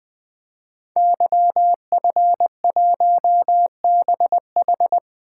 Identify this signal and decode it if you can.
Morse code, heard as a keyed tone, about 20 wpm, 700 Hz. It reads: YF1BH
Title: Code Groups - 5 Letter:Numbers - 20wpm